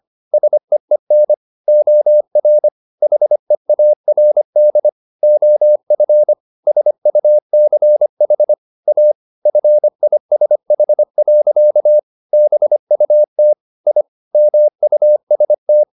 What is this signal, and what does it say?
Morse code, heard as a keyed tone, about 25 wpm, 600 Hz. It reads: SEEN OR HEARD OF SUCH A FISH. BUT I MUST